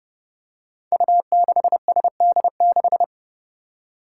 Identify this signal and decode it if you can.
Morse code, heard as a keyed tone, about 30 wpm, 700 Hz. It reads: U6SD6
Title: Five Challenging Letter-Number Combinations 30wpm